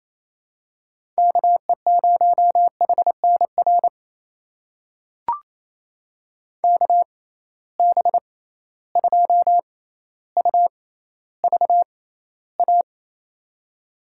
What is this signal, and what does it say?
Morse code, heard as a keyed tone, about 28 wpm, spaced as for 14 wpm, 700 Hz. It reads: KE0HNR KB2UVA
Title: USA Call Signs - 6 Characters - 6x Character Spacing - 28wpm